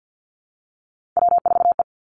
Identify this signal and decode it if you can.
Morse code, heard as a keyed tone, about 50 wpm, 700 Hz. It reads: W4E
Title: Three Challenging Letter-Number Combinations 50wpm